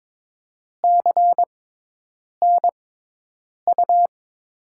Morse code, heard as a keyed tone, about 22 wpm, 700 Hz. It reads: C N U
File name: Top 90 QSO Elements - 6x Character Spacing - 22wpm